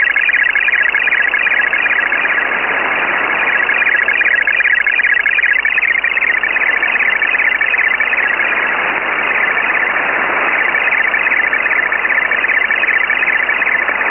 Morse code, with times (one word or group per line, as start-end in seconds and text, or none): none